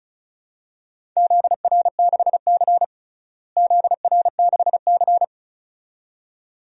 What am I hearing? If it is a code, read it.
Morse code, heard as a keyed tone, about 34 wpm, 700 Hz. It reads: ZR6C ZR6C
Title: DXCC-100 Call Signs - 4 Characters - Repeated 2x - 35wpm